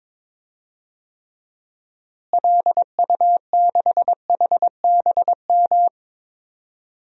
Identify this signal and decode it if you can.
Morse code, heard as a keyed tone, about 22 wpm, 700 Hz. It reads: LU6HBM
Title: DXCC-100 Call Signs - 6 Characters - Repeated 2x - 22wpm